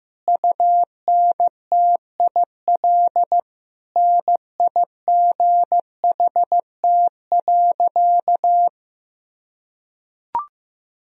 Morse code, heard as a keyed tone, about 15 wpm, 700 Hz. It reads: UNTIL NIGHT.
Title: Sentences from Top 1000 Words - Review Part 1 - 15wpm